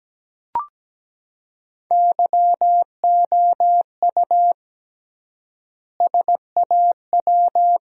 Morse code, heard as a keyed tone, about 17 wpm, 700 Hz. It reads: YOU SAW
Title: 3x Word-Spacing - Sentences from Top 300 Words - 17wpm